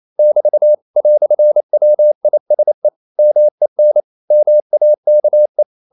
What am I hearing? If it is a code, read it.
Morse code, heard as a keyed tone, about 28 wpm, 600 Hz. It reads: <BT> "WISE MEN MAKE
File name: CW / Practice / EnglishQuotations0228